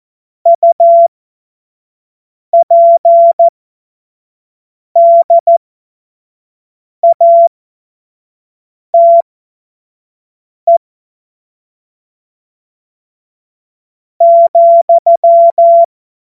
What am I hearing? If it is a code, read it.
Morse code, heard as a keyed tone, about 14 wpm, spaced as for 4 wpm, 680 Hz. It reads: UPDATE,
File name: Morsekurs / Witze / Witz 20. Juli 2021, 25 F70